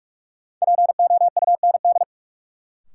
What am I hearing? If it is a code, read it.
Morse code, heard as a keyed tone, about 42 wpm, 700 Hz. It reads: POUND
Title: Sentences from Top 1000 Words - Review Part 2 - New Word Drill - 45wpm